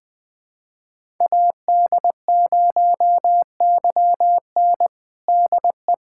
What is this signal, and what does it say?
Morse code, heard as a keyed tone, about 20 wpm, 700 Hz. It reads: AD0YN DE
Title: USA Callsign DE Callsign 20wpm